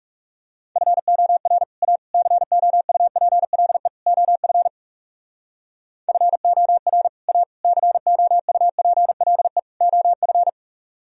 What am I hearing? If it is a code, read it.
Morse code, heard as a keyed tone, about 40 wpm, 700 Hz. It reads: FOR A COUPLE OF FOR A COUPLE OF